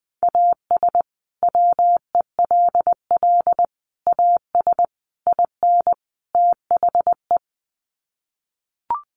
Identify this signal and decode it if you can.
Morse code, heard as a keyed tone, about 20 wpm, 700 Hz. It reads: AS WELL AS IN THE
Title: Sets of 5 Words Encore 20wpm